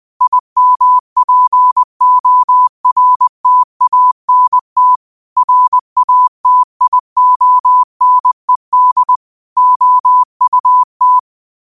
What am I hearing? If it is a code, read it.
Morse code, heard as a keyed tone, about 20 wpm, 1000 Hz. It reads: IMPORTANT RATIONED OUT